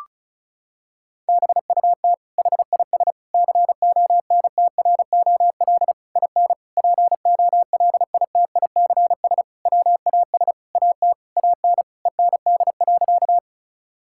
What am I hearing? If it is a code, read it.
Morse code, heard as a keyed tone, about 36 wpm, 700 Hz. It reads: BUT HIS CONTROL IN POLITICS WAS AT AN END.
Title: Sentences from Top 1000 Words - Review Part 6 - 35wpm